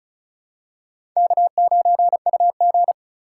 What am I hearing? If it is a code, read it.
Morse code, heard as a keyed tone, about 36 wpm, 700 Hz. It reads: K9UG
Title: USA Call Signs - 4 Characters - 3x Character Spacing - 35wpm